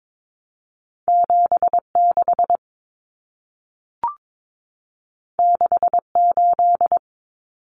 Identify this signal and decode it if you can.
Morse code, heard as a keyed tone, about 22 wpm, 700 Hz. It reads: 76 68
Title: Numbers - 2-Digit 22wpm